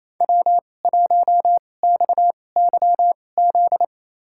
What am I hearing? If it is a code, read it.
Morse code, heard as a keyed tone, about 28 wpm, spaced as for 20 wpm, 700 Hz. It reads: W1XYZ
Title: USA Call Signs - 5 Characters - 2x Character Spacing - 28wpm